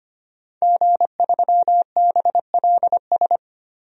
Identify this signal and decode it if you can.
Morse code, heard as a keyed tone, about 25 wpm, 700 Hz. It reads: G3BLS
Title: DXCC-100 Call Signs - 5 Characters - 25wpm